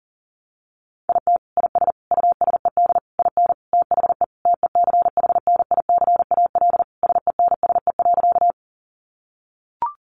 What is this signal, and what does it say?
Morse code, heard as a keyed tone, about 37 wpm, 700 Hz. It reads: IT IS USED IN THE TECHNICAL SENSE.
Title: Sentences from Top 1100 Words 40wpm